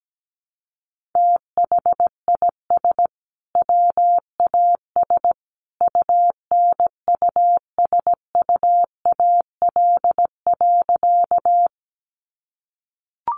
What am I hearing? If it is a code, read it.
Morse code, heard as a keyed tone, about 17 wpm, 700 Hz. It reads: THIS WAS UNUSUAL.
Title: Sentences from Top 1400 Words 17wpm